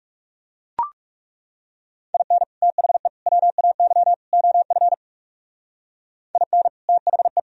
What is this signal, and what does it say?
Morse code, heard as a keyed tone, about 39 wpm, 700 Hz. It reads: IN THE WAY OF IN THE